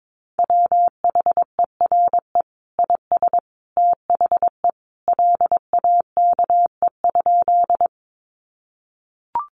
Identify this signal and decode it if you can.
Morse code, heard as a keyed tone, about 22 wpm, 700 Hz. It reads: WHERE IS THE LAKE?
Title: Sentences from Top 1100 Words 22wpm